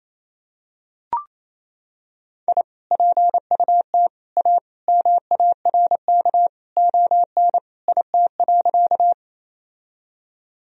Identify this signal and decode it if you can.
Morse code, heard as a keyed tone, about 28 wpm, 700 Hz. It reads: I PUT A MARK ON IT.
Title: Sentences from Top 1000 Words - Review Part 1 - 28wpm